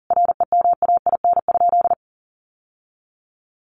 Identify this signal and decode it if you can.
Morse code, heard as a keyed tone, about 40 wpm, 700 Hz. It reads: REMAIN?
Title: Sentences from Top 1200 Words 40wpm